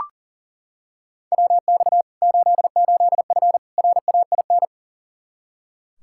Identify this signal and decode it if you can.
Morse code, heard as a keyed tone, about 40 wpm, 700 Hz. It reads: WX 88F RAIN
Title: Practice QSO - Exchange 2 - Info - 40wpm